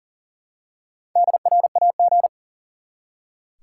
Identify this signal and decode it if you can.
Morse code, heard as a keyed tone, about 39 wpm, 700 Hz. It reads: DRAG
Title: Four Letter Words - 40wpm